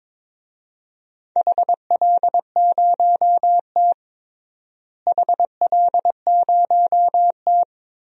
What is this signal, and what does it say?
Morse code, heard as a keyed tone, about 21 wpm, 700 Hz. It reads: HL0T HL0T
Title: DXCC-100 Call Signs - 4 Characters - Repeated 3x - 22wpm